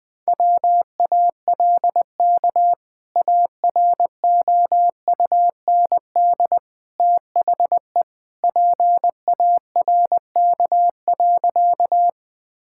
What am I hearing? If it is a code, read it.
Morse code, heard as a keyed tone, about 19 wpm, 700 Hz. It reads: WALK AROUND THE PARK.
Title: Sentences with 2-Word Phrases - 20wpm